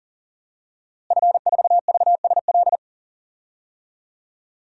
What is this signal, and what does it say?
Morse code, heard as a keyed tone, about 40 wpm, 700 Hz. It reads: F4VSL
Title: DXCC-100 Call Signs - 5 Characters - Repeated 2x - 40wpm